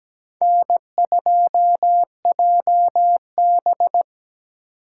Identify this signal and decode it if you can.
Morse code, heard as a keyed tone, about 17 wpm, 700 Hz. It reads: N2JB
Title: USA Call Signs - 2x Character Spacing - 17wpm